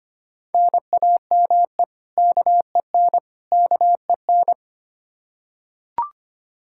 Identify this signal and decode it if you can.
Morse code, heard as a keyed tone, about 25 wpm, 700 Hz. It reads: NAME KEN KEN
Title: Practice QSO - Exchange 1 - Beginning - 25wpm